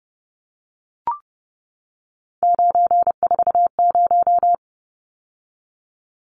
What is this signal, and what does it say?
Morse code, heard as a keyed tone, about 30 wpm, 700 Hz. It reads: 940
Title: Numbers - 3-Digit 30wpm